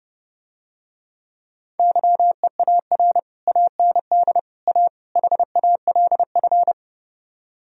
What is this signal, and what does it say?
Morse code, heard as a keyed tone, about 30 wpm, 700 Hz. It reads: YEAR AND A HALF